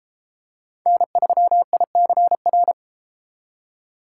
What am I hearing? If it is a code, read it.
Morse code, heard as a keyed tone, about 34 wpm, 700 Hz. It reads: N3ICR